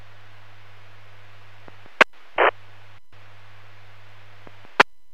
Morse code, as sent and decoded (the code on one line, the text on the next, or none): none